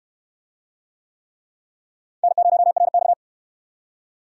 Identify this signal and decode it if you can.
Morse code, heard as a keyed tone, about 62 wpm, 700 Hz. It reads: N0RX